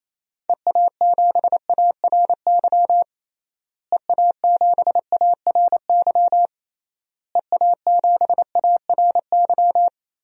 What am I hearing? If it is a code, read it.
Morse code, heard as a keyed tone, about 28 wpm, 700 Hz. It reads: EA7ARY EA7ARY EA7ARY
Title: DXCC-100 Call Signs - 6 Characters - Repeated 3x - 28wpm